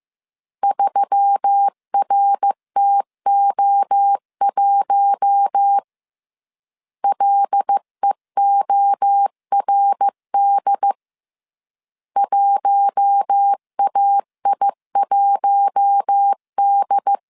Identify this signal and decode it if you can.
Morse code, heard as a keyed tone, about 15 wpm, 800 Hz. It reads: 3RTO1 LEORD 1AI1D
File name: CW / DET / 1 6 1500C S B 14wpm 2958